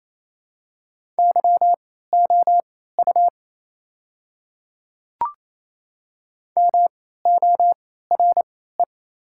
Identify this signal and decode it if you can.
Morse code, heard as a keyed tone, about 28 wpm, spaced as for 16 wpm, 700 Hz. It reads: YOU MORE
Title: 3x Character Spacing - Top 200 Words - 28wpm